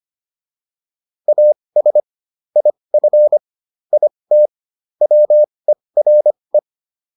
Morse code, heard as a keyed tone, about 25 wpm, spaced as for 20 wpm, 600 Hz. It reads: AS IF IT WERE